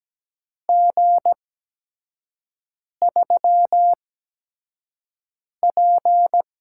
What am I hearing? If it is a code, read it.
Morse code, heard as a keyed tone, about 17 wpm, spaced as for 5 wpm, 700 Hz. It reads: G3P
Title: USA Call Signs - 4 Characters - 8x Character Spacing - 17wpm